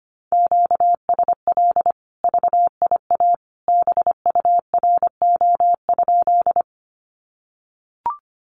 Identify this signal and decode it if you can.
Morse code, heard as a keyed tone, about 25 wpm, 700 Hz. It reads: QSL VIA BURO?
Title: Common QSO Phrases 25wpm